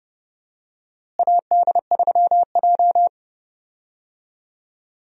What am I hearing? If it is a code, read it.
Morse code, heard as a keyed tone, about 30 wpm, 700 Hz. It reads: AD3J
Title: USA Call Signs - 4 Characters - Repeated 3x - 30wpm